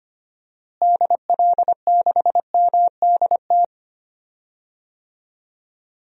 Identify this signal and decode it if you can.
Morse code, heard as a keyed tone, about 25 wpm, 700 Hz. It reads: DL6MDT
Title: DXCC-100 Call Signs - 6 Characters - Repeated 2x - 25wpm